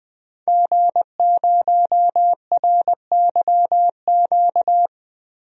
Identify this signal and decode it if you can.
Morse code, heard as a keyed tone, about 21 wpm, 700 Hz. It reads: G0RYQ
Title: DXCC-100 Call Signs - 5 Characters - Repeated 3x - 20wpm